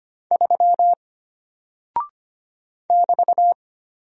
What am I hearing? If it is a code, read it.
Morse code, heard as a keyed tone, about 25 wpm, 700 Hz. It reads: 3 <BT>
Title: Single Character (Letter:Number:Punctuation:Prosign) Alt 25wpm